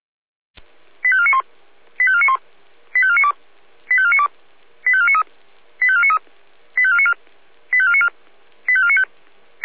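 Five-tone selective call (ZVEI), heard as quick runs of five tones, tones of about 70 ms. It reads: 85491 85491 85492 85492 85493 85493 85494 85494 85495
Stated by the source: Five-tone ZVEI selective call received on an SDR.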